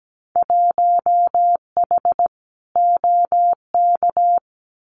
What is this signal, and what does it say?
Morse code, heard as a keyed tone, about 17 wpm, 700 Hz. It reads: 1H OK
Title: Winter Field Day Exchanges 17wpm